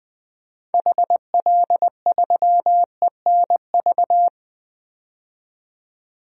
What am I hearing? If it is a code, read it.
Morse code, heard as a keyed tone, about 20 wpm, 700 Hz. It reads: HL3ENV
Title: DXCC-100 Call Signs - 6 Characters - 20wpm